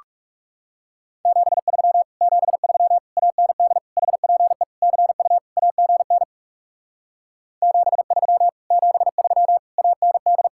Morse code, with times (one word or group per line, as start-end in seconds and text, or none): 1.3-2.0 73
2.2-3.0 73
3.2-3.8 AND
4.0-4.6 HPE
4.8-5.4 CU
5.6-6.2 AGN
7.6-8.5 73
8.7-9.6 73
9.8-10.5 AND